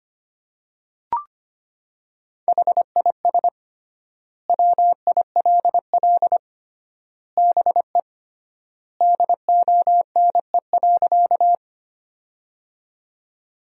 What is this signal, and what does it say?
Morse code, heard as a keyed tone, about 25 wpm, 700 Hz. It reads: HIS WILL BE DONE.
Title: 3x Word-Spacing - Sentences from Top 500 Words - 25wpm